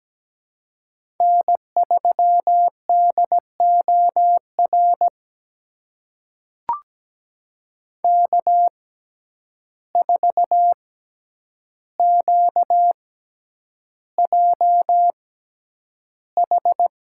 Morse code, heard as a keyed tone, about 17 wpm, spaced as for 9 wpm, 700 Hz. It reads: N3DOR K4QJH
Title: USA Call Signs - 5 Characters - 6x Character Spacing - 17wpm